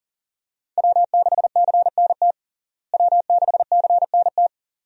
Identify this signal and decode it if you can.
Morse code, heard as a keyed tone, about 40 wpm, 700 Hz. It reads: W6CNT W6CNT
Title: USA Call Signs - 5 Characters - Repeated 2x - 40wpm